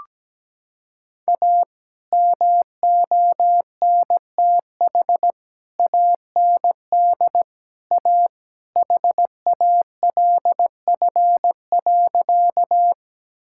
Morse code, heard as a keyed tone, about 17 wpm, 700 Hz. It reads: A MONTH AND A HALF.
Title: Sentences from Top 1000 Words - Review Part 5 - 17wpm